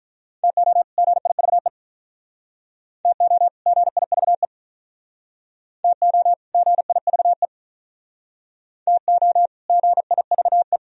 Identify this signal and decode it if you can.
Morse code, heard as a keyed tone, about 46 wpm, 700 Hz. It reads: TO GIVE TO GIVE TO GIVE TO GIVE